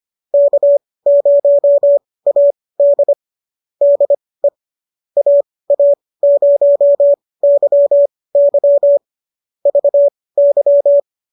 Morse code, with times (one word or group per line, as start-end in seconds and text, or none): 0.3-3.1 K0AD
3.8-4.5 DE
5.2-9.0 AA0YY
9.7-11.0 VY